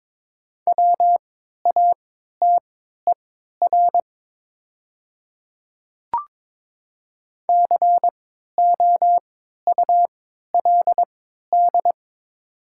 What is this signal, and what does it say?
Morse code, heard as a keyed tone, about 22 wpm, spaced as for 13 wpm, 700 Hz. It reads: WATER COULD
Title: 3x Character Spacing - Top 100 Words - 22wpm